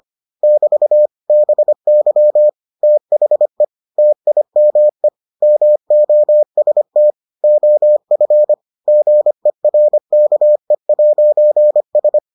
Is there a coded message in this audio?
Yes: Morse code from 0.4 to 12.2 s, heard as a keyed tone, about 25 wpm, 600 Hz. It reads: <BT> BY THE TIME MOST OF GERKE'S